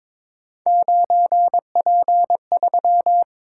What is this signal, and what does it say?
Morse code, heard as a keyed tone, about 23 wpm, 700 Hz. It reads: 9P3
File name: practice / 22wpm / Three Letter-Number Combinations - 22wpm